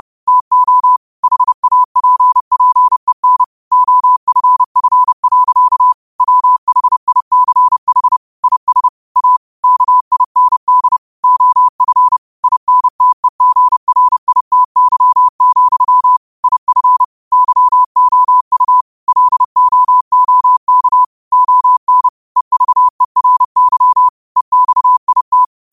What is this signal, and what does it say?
Morse code, heard as a keyed tone, about 30 wpm, 1000 Hz. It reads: TO HAPPEN OFF. WHICH IS A KIND OF INTEGRITY, IF YOU LOOK ON EVERY EXIT
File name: qotdfiles / 30wpm